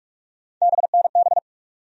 Morse code, heard as a keyed tone, about 46 wpm, 700 Hz. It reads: BNB
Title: USA Call Sign Postfixes - 3 Letters 45wpm